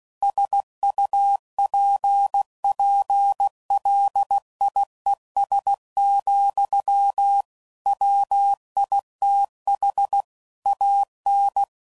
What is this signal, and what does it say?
Morse code, heard as a keyed tone, about 16 wpm, 800 Hz. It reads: SUPPLIES, WITH AN
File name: CW Code Groups @15WPM / Technical Subjects / PwrSupply